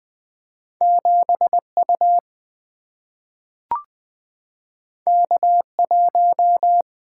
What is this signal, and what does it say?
Morse code, heard as a keyed tone, about 20 wpm, 700 Hz. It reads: 7U K1
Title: Single-Letter Single-Number 20wpm